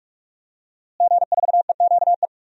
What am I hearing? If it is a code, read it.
Morse code, heard as a keyed tone, about 45 wpm, 700 Hz. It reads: G4EQE
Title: DXCC-100 Call Signs - 5 Characters - Repeated 2x - 45wpm